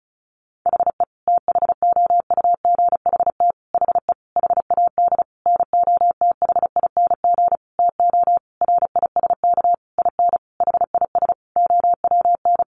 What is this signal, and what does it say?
Morse code, heard as a keyed tone, about 34 wpm, 700 Hz. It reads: HE THOUGHT HE HAD NOTHING TO RISK IN HIS OWN
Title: Sentences from Top 600 Words 35wpm